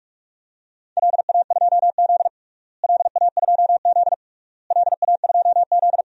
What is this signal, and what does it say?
Morse code, heard as a keyed tone, about 46 wpm, 700 Hz. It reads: LA2Z LA2Z LA2Z
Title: DXCC-100 Call Signs - 4 Characters - Repeated 3x - 45wpm